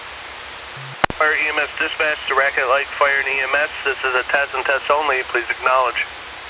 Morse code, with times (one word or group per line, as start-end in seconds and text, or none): none